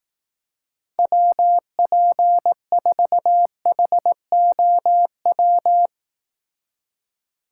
Morse code, heard as a keyed tone, about 18 wpm, 700 Hz. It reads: WP4HOW
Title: Speed Racing - USA 2x3 Call Signs 15wpm